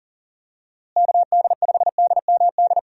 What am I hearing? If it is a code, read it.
Morse code, heard as a keyed tone, about 40 wpm, 700 Hz. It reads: KD5DMD
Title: USA Call Signs - 6 Characters - Repeated 2x - 40wpm